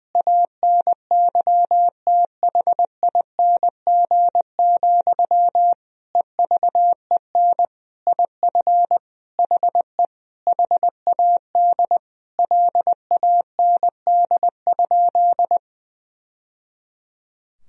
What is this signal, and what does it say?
Morse code, heard as a keyed tone, about 20 wpm, 700 Hz. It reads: ANYTHING, EVEN IF HE HAD LAND?
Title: Sentences from Top 1000 Words - Review Part 1 - 20wpm